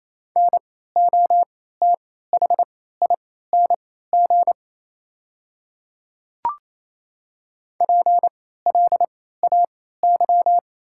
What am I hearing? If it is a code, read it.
Morse code, heard as a keyed tone, about 28 wpm, spaced as for 16 wpm, 690 Hz. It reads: NOTHING PLAY
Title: 3x Character Spacing - Top 500 Words - 28wpm